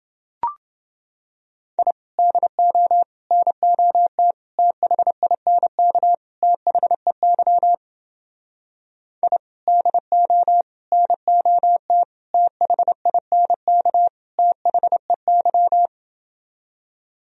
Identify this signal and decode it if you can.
Morse code, heard as a keyed tone, about 30 wpm, 700 Hz. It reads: I DO NOT THINK THEY I DO NOT THINK THEY